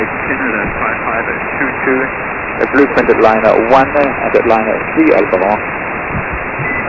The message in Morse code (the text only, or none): none